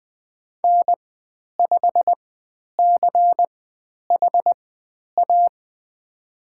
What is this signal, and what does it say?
Morse code, heard as a keyed tone, about 20 wpm, spaced as for 10 wpm, 700 Hz. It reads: N5CHA
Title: 095 Lesson 16g - Letter c - Call Signs 20wpm